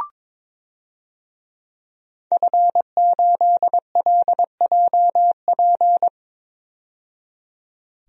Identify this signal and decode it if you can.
Morse code, heard as a keyed tone, about 22 wpm, 700 Hz. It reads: F8LJP
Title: DXCC-100 Call Signs - 5 Characters - 22wpm